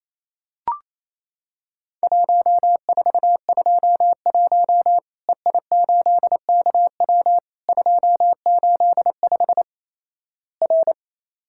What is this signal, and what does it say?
Morse code, heard as a keyed tone, about 28 wpm, 700 Hz. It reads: 1421 EI8KW 285 R
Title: Worked All Europe DX Contest - QTC Exchange - 28wpm